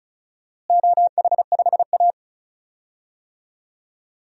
Morse code, heard as a keyed tone, about 36 wpm, 700 Hz. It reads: OH5A
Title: DXCC-100 Call Signs - 4 Characters - Repeated 3x - 35wpm